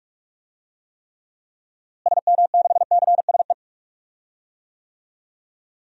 Morse code, heard as a keyed tone, about 46 wpm, 700 Hz. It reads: SM6CSE
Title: DXCC-100 Call Signs - 6 Characters - Repeated 2x - 45wpm